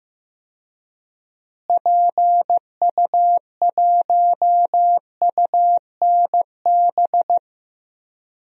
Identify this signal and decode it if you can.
Morse code, heard as a keyed tone, about 15 wpm, 700 Hz. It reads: PU1UNB
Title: DXCC-100 Call Signs - 6 Characters - Repeated 2x - 15wpm